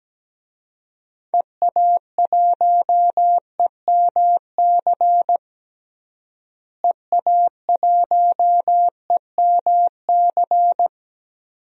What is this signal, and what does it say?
Morse code, heard as a keyed tone, about 17 wpm, 700 Hz. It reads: EA1EMC EA1EMC
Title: DXCC-100 Call Signs - 6 Characters - Repeated 3x - 17wpm